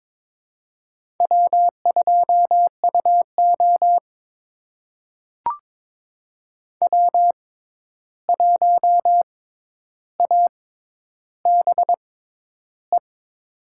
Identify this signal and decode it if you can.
Morse code, heard as a keyed tone, about 22 wpm, spaced as for 10 wpm, 700 Hz. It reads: W2UO W1ABE
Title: USA Call Signs - 6x Character Spacing - 22wpm